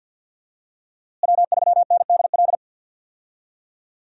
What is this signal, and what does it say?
Morse code, heard as a keyed tone, about 52 wpm, 700 Hz. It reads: W3NDL
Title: USA Call Signs - 5 Characters - Repeated 3x - 50wpm